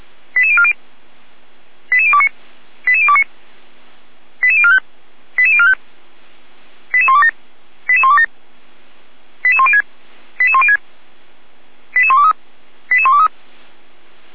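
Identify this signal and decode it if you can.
Five-tone selective call (ZVEI), heard as quick runs of five tones, tones of about 70 ms. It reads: 88040 88029 88029 88045 88045 88127 88127 88106 88106 88123 88123
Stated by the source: ZVEI five-tone call received by SDR at feuerwehr-orlamuende.org (fire brigade Orlamünde).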